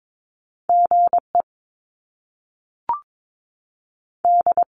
Morse code, heard as a keyed tone, about 22 wpm, spaced as for 3 wpm, 700 Hz. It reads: G ED
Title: Top Two Letter Combinations in Words 22wpm